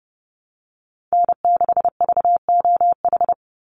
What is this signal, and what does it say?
Morse code, heard as a keyed tone, about 30 wpm, 700 Hz. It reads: N6VOH
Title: Call Signs - 30wpm